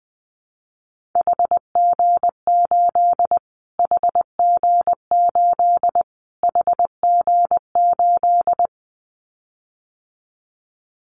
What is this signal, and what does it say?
Morse code, heard as a keyed tone, about 20 wpm, 700 Hz. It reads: HG8 HG8 HG8
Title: Callsign Prefixes for Top 25 DXCC - Repeated 3x - 20wpm